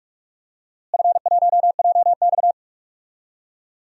Morse code, heard as a keyed tone, about 43 wpm, 700 Hz. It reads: F1JX
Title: DXCC-100 Call Signs - 4 Characters - Repeated 2x - 45wpm